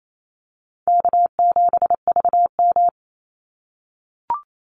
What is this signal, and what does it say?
Morse code, heard as a keyed tone, about 28 wpm, 700 Hz. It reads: K7VM
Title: Easy Call Signs - 28wpm